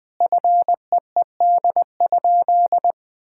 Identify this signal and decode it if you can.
Morse code, heard as a keyed tone, about 21 wpm, 700 Hz. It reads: FEED?
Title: Sentences from Top 1600 Words 20wpm